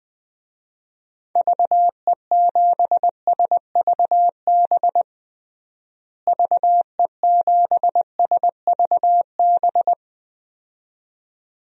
Morse code, heard as a keyed tone, about 21 wpm, 700 Hz. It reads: VE7SVB VE7SVB
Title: DXCC-100 Call Signs - 6 Characters - Repeated 2x - 20wpm